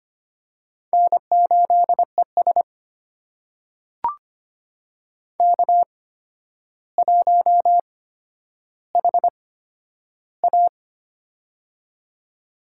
Morse code, heard as a keyed tone, about 25 wpm, spaced as for 11 wpm, 700 Hz. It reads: N8ES K1HA
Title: USA Call Signs - 4 Characters - 8x Character Spacing - 25wpm